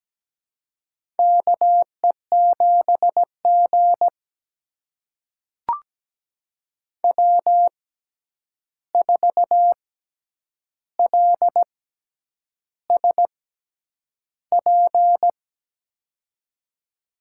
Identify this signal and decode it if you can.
Morse code, heard as a keyed tone, about 17 wpm, spaced as for 8 wpm, 700 Hz. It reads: KE7G W4LSP
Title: USA Call Signs - 6x Character Spacing - 17wpm